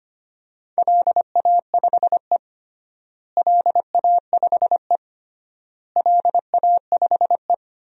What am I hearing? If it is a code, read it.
Morse code, heard as a keyed tone, about 25 wpm, 700 Hz. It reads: LA5E LA5E LA5E
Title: DXCC-100 Call Signs - 4 Characters - Repeated 3x - 25wpm